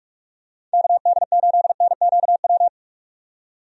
Morse code, heard as a keyed tone, about 46 wpm, 700 Hz. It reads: KD8NQW
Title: USA Call Signs - 6 Characters - 45wpm